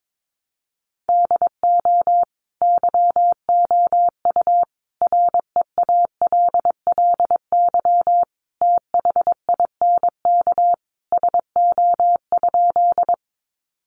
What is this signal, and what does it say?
Morse code, heard as a keyed tone, about 22 wpm, 700 Hz. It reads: DO YOU REALLY THINK SO?
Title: Sentences from Top 600 Words 22wpm